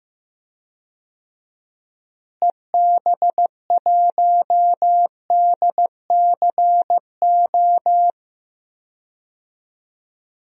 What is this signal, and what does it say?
Morse code, heard as a keyed tone, about 15 wpm, 700 Hz. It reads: EB1DCO
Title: DXCC-100 Call Signs - 6 Characters - Repeated 3x - 15wpm